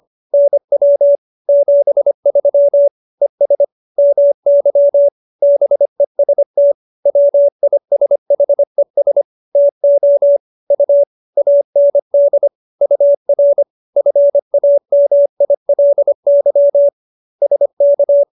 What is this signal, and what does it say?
Morse code, heard as a keyed tone, about 25 wpm, 570 Hz. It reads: NW 73 ES MY BEST WISHES TO U AND UR FAMILY SK